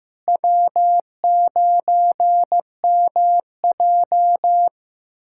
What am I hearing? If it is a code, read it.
Morse code, heard as a keyed tone, about 15 wpm, 700 Hz. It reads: W9MJ
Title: USA Callsign DE Callsign - 4 Characters - 15wpm